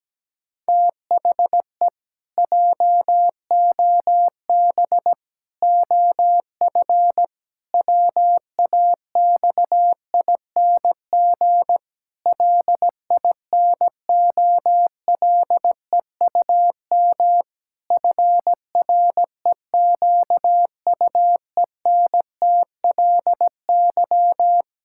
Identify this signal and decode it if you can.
Morse code, heard as a keyed tone, about 17 wpm, 700 Hz. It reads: THE JOB OF WAXING LINOLEUM FREQUENTLY
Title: Pangram Challenge v2 - 17wpm